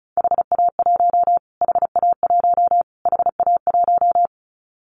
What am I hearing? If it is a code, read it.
Morse code, heard as a keyed tone, about 34 wpm, 700 Hz. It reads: HA1 HA1 HA1
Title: Callsign Prefixes for Top 25 DXCC - Repeated 3x - 35wpm